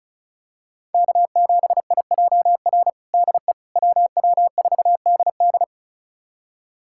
Morse code, heard as a keyed tone, about 34 wpm, 700 Hz. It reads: K7IJR DE WW4DD
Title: USA Callsign DE Callsign - 5 Characters - 35wpm